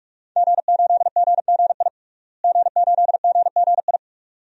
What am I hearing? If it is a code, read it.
Morse code, heard as a keyed tone, about 46 wpm, 700 Hz. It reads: G8GGI G8GGI
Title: DXCC-100 Call Signs - 5 Characters - Repeated 2x - 45wpm